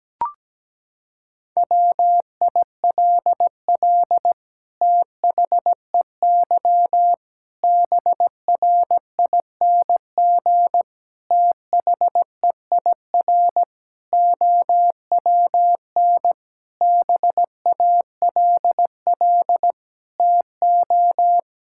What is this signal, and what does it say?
Morse code, heard as a keyed tone, about 17 wpm, 700 Hz. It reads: WILL THEY BRING THEIR OWN BALL TO
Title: Sentences from Top 1000 Words - Review Part 6 - 17wpm